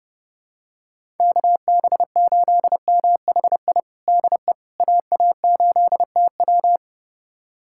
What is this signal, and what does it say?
Morse code, heard as a keyed tone, about 30 wpm, 700 Hz. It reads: KB8MHI DE AA8TW
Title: USA Callsign DE Callsign 30wpm